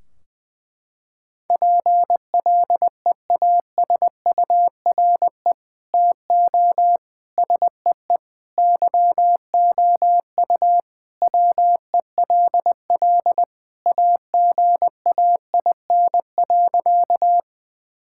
Morse code, heard as a keyed tone, about 20 wpm, 700 Hz. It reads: PLEASURE TO SEE YOU WELL AGAIN.
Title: Sentences from Top 1600 Words 20wpm